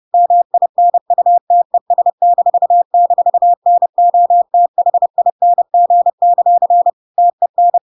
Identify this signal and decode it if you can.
Morse code, heard as a keyed tone, about 30 wpm, 700 Hz. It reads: MINUTES--NOTHING; TEN
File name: Pinocchio / Pinocchio ch1 30 30 ik1hge